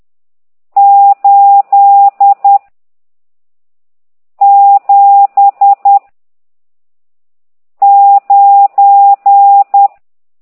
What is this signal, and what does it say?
Morse code, heard as a keyed tone, about 10 wpm, 800 Hz. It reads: 8 7 9
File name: ct2idw / exerc14